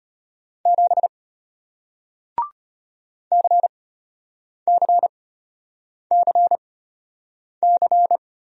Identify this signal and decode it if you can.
Morse code, heard as a keyed tone, about 34 wpm, spaced as for 8 wpm, 700 Hz. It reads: 7 CCCC